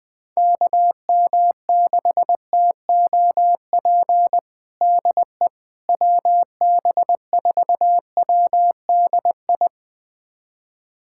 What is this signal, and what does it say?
Morse code, heard as a keyed tone, about 20 wpm, 700 Hz. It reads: KM6TOP DE WB4WDI
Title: USA Callsign DE Callsign - 6 Characters - 20wpm